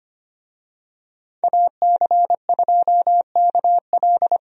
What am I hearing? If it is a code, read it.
Morse code, heard as a keyed tone, about 25 wpm, 690 Hz. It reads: AC2KL
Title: USA Call Signs - 5 Characters - 25wpm